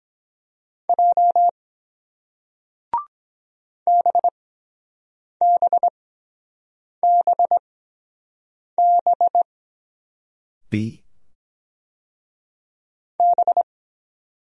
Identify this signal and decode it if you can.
Morse code, heard as a keyed tone, about 23 wpm, spaced as for 7 wpm, 700 Hz. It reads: J BBBB B